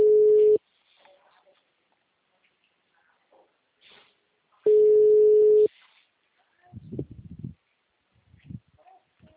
Telephone call progress tone (Europe, 425 Hz), heard as a telephone network's tone, about 420 Hz. Ringback tone.